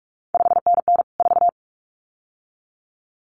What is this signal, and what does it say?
Morse code, heard as a keyed tone, about 46 wpm, 700 Hz. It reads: 5NN 4
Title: CQ WW DX Contest Exchange - 45wpm